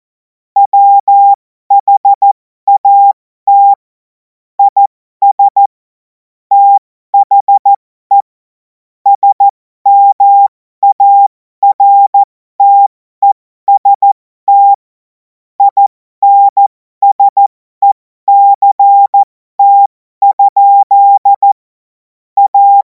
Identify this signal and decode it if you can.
Morse code, heard as a keyed tone, about 14 wpm, spaced as for 12 wpm, 800 Hz. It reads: WHAT IS THE SMARTEST INSECT? A
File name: Morsekurs / Witze